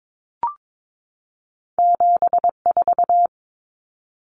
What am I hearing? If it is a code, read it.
Morse code, heard as a keyed tone, about 22 wpm, 700 Hz. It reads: 74
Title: Common Cut Numbers - 2-Digit 22wpm